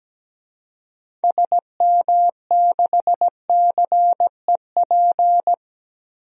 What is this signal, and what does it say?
Morse code, heard as a keyed tone, about 17 wpm, 700 Hz. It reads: SM6CEP
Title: DXCC-100 Call Signs - 6 Characters - Repeated 3x - 17wpm